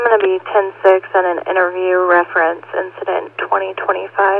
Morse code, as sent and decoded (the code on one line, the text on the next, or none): none